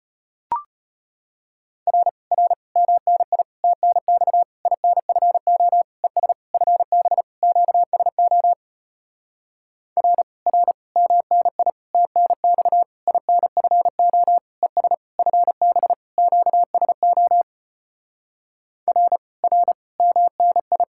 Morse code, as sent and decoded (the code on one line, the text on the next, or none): .-. / .-. / -- -. .. / - -. -..- / .. -. ..-. --- / . ... / ..-. -... / --.- ... --- / .-. / .-. / -- -. .. / - -. -..- / .. -. ..-. --- / . ... / ..-. -... / --.- ... --- / .-. / .-. / -- -. ..
R R MNI TNX INFO ES FB QSO R R MNI TNX INFO ES FB QSO R R MNI